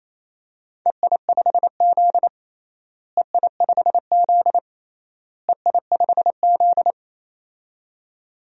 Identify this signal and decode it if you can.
Morse code, heard as a keyed tone, about 28 wpm, 700 Hz. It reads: EI5Z EI5Z EI5Z
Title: DXCC-100 Call Signs - 4 Characters - Repeated 3x - 28wpm